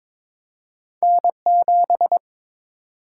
Morse code, heard as a keyed tone, about 22 wpm, 700 Hz. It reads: N7
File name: practice / 22wpm / Two Letter-Number Combinations - 22wpm